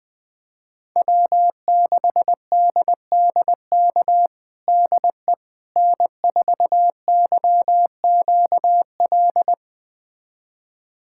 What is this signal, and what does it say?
Morse code, heard as a keyed tone, about 21 wpm, 700 Hz. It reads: W6DDK DE N4YQL
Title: USA Callsign DE Callsign - 5 Characters - 20wpm